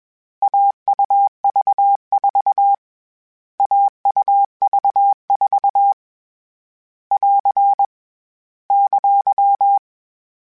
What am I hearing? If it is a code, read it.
Morse code, heard as a keyed tone, about 21 wpm, 800 Hz. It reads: AUV4 AUV4 <AR> !
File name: Difficult Combos / Difficult Combos-mp3s / AUV AUV4 @20